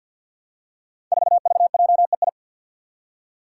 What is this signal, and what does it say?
Morse code, heard as a keyed tone, about 48 wpm, 700 Hz. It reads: 4VJEI
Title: Five Challenging Letter-Number Combinations 50wpm